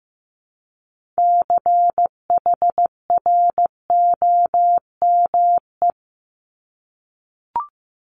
Morse code, heard as a keyed tone, about 15 wpm, 700 Hz. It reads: CHROME
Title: Multi-letter Phonemes Repeated 3x 15wpm